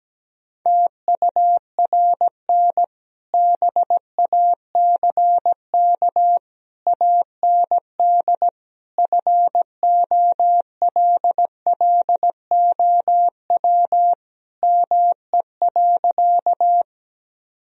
Morse code, heard as a keyed tone, about 17 wpm, 700 Hz. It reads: TURN BACK AND FOLLOW ME.
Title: Sentences from Top 1000 Words - Review Part 1 - 17wpm